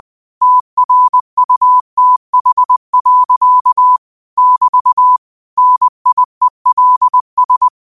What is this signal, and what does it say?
Morse code, heard as a keyed tone, about 20 wpm, 1000 Hz. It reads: TRUTH. <BT> NIELS
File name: qotdfiles / 20wpm